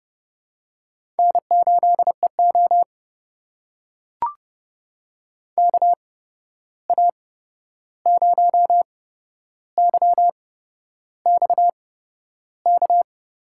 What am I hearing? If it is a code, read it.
Morse code, heard as a keyed tone, about 31 wpm, spaced as for 11 wpm, 700 Hz. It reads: N8EO KA0YXK
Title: USA Call Signs - 8x Character Spacing - 30wpm